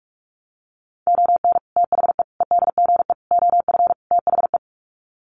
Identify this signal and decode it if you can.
Morse code, heard as a keyed tone, about 44 wpm, 700 Hz. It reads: ON THE EDGE OF THE
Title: Sets of 5 Words 45wpm